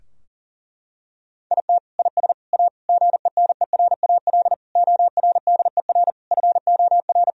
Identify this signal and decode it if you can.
Morse code, heard as a keyed tone, about 40 wpm, 700 Hz. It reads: IT IS A GENERAL ORDER FOR
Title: Sentences from Top 1000 Words - Review Part 3 - 40wpm